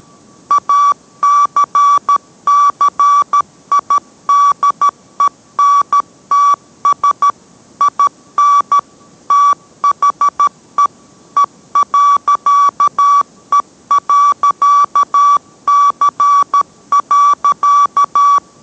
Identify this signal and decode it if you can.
Morse code, heard as a keyed tone, about 13 wpm, 1150 Hz. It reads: ACCIDENTS IN THE E.E.C.